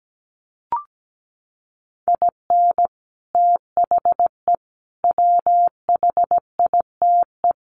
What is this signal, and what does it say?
Morse code, heard as a keyed tone, about 17 wpm, 700 Hz. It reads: IN THE WHITE
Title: Sets of 4 Words 17wpm